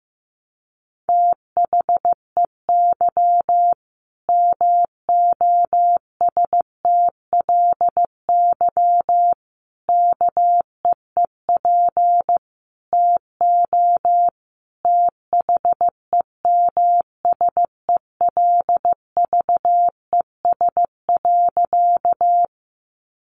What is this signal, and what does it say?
Morse code, heard as a keyed tone, about 15 wpm, 700 Hz. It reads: THEY MOSTLY KEEP TO THEMSELVES.
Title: Sentences from Top 1300 Words 15wpm